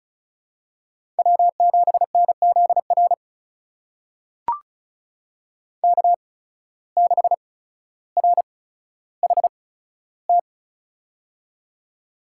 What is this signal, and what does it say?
Morse code, heard as a keyed tone, about 36 wpm, spaced as for 15 wpm, 700 Hz. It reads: W7NZR K6RHT
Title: USA Call Signs - 8x Character Spacing - 35wpm